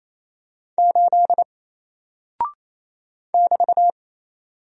Morse code, heard as a keyed tone, about 28 wpm, 700 Hz. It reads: 8 <BT>
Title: Single Character (Letter:Number:Punctuation:Prosign) Alt 28wpm